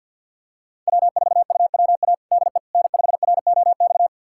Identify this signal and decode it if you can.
Morse code, heard as a keyed tone, about 51 wpm, 700 Hz. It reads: W4UWA DE N5ROX
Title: USA Callsign DE Callsign - 5 Characters - 50wpm